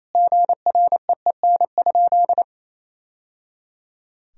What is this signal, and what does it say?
Morse code, heard as a keyed tone, about 28 wpm, 700 Hz. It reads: GREEN?
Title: Sentences from Top 1000 Words - Review Part 2 - 28wpm